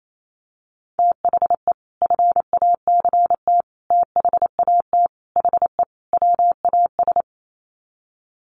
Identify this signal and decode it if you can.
Morse code, heard as a keyed tone, about 28 wpm, 700 Hz. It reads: THE FACT THAT HE WAS
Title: Sets of 5 Words 28wpm